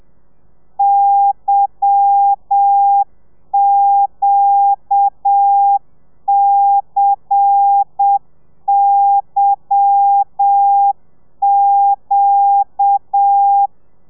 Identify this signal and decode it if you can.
Morse code, heard as a keyed tone, about 7 wpm, 800 Hz. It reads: YQCYQ
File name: telegrafia / i2tfj / Lez10 30